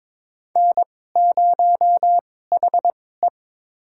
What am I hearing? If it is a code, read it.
Morse code, heard as a keyed tone, about 22 wpm, spaced as for 16 wpm, 700 Hz. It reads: N0HE
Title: USA Call Signs - 4 Characters - 2x Character Spacing - 22wpm